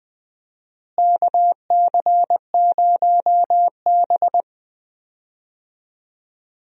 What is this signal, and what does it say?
Morse code, heard as a keyed tone, about 20 wpm, 700 Hz. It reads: KC0B
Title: USA Call Signs - 4 Characters - Repeated 2x - 20wpm